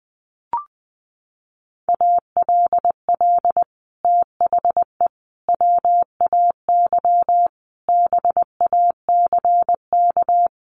Morse code, heard as a keyed tone, about 19 wpm, 700 Hz. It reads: ALL THE WAY BACK
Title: Sets of 5 Words Encore 20wpm